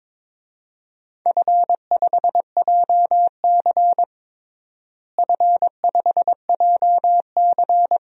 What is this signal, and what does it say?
Morse code, heard as a keyed tone, about 22 wpm, 700 Hz. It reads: F5JC F5JC
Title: DXCC-100 Call Signs - 4 Characters - Repeated 3x - 22wpm